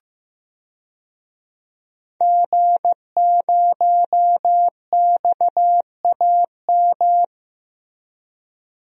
Morse code, heard as a keyed tone, about 15 wpm, 700 Hz. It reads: G0XAM
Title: DXCC-100 Call Signs - 5 Characters - Repeated 3x - 15wpm